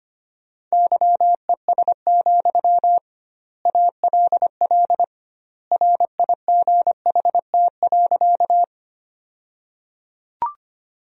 Morse code, heard as a keyed tone, about 25 wpm, 700 Hz. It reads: YES, ALL RIGHT.
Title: 2x Word-Spacing - Sentences from Top 500 Words - 25wpm